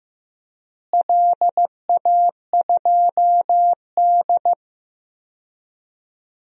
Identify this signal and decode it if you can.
Morse code, heard as a keyed tone, about 15 wpm, 700 Hz. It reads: LA2D
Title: DXCC-100 Call Signs - 4 Characters - Repeated 3x - 15wpm